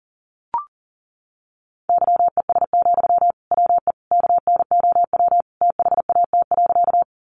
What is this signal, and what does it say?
Morse code, heard as a keyed tone, about 40 wpm, 700 Hz. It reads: YES, WE KNOW THAT.
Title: Sentences from Top 500 Words 40wpm